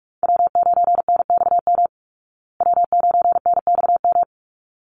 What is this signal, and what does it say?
Morse code, heard as a keyed tone, about 44 wpm, 700 Hz. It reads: W9NXM W9NXM
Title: Call Signs Repeated 3x - 45wpm